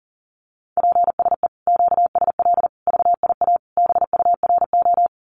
Morse code, heard as a keyed tone, about 40 wpm, 700 Hz. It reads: PSE QSL VIA BURO
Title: Common QSO Phrases 40wpm